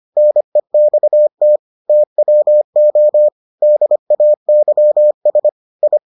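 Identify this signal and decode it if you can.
Morse code, heard as a keyed tone, about 25 wpm, 600 Hz. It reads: NEXT TWO DAYS I
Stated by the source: Morse edition of the Foundations of Amateur Radio episodes.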